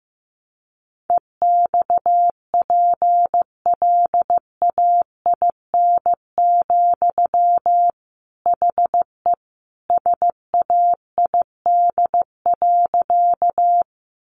Morse code, heard as a keyed tone, about 15 wpm, 700 Hz. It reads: EXPLAIN, HE SAID.
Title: Sentences from Top 1100 Words 15wpm